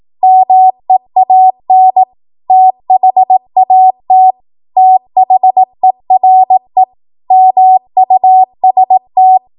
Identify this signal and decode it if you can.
Morse code, heard as a keyed tone, about 18 wpm, 750 Hz. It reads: MEAN THAT THERE MUST
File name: Morse / 18WPM